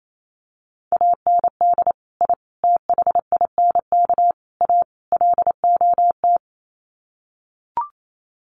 Morse code, heard as a keyed tone, about 28 wpm, 700 Hz. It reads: AND I THINK A LOT
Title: Sets of 5 Words Encore2 28wpm